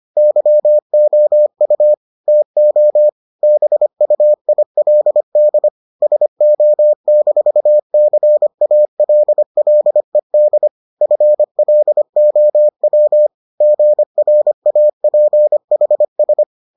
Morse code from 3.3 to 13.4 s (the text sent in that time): BUILD SO-CALLED FLOW